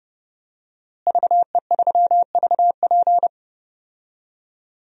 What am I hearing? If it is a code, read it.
Morse code, heard as a keyed tone, about 32 wpm, 700 Hz. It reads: VE3VP
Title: DXCC-100 Call Signs - 5 Characters - Repeated 2x - 30wpm